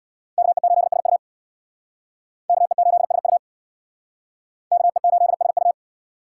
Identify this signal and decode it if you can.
Morse code, heard as a keyed tone, about 62 wpm, 700 Hz. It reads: KE8SU KE8SU KE8SU
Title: Speed Racing - USA Call Signs 50wpm